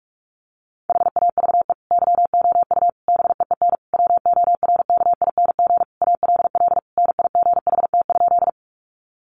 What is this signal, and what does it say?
Morse code, heard as a keyed tone, about 44 wpm, 700 Hz. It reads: HAVE YOU BEEN WORKING ALL NIGHT?
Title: Sentences from Top 700 Words 45wpm